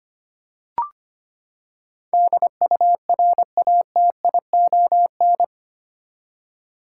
Morse code, heard as a keyed tone, about 25 wpm, 700 Hz. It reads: DURATION
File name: Eight Letter Words - 25wpm